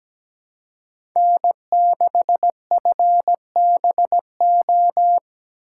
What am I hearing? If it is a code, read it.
Morse code, heard as a keyed tone, about 17 wpm, 700 Hz. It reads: N6FBO
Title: USA Call Signs - 5 Characters - 3x Character Spacing - 17wpm